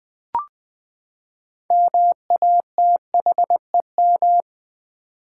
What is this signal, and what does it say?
Morse code, heard as a keyed tone, about 20 wpm, 700 Hz. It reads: MATHEM
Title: Build A Word - 6 Letters At A Time - 12-Letter Words - 20wpm